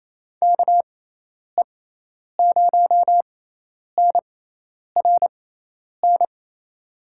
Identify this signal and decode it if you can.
Morse code, heard as a keyed tone, about 28 wpm, spaced as for 10 wpm, 700 Hz. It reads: KE0NRN
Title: USA Call Signs - 6 Characters - 6x Character Spacing - 28wpm